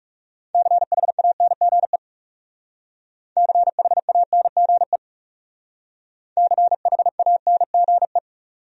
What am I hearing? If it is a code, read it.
Morse code, heard as a keyed tone, about 39 wpm, 700 Hz. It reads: CHANGE CHANGE CHANGE